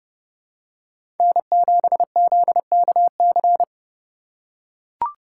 Morse code, heard as a keyed tone, about 32 wpm, 700 Hz. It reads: N7ZKC
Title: USA Call Signs - 5 Characters - 3x Character Spacing - 30wpm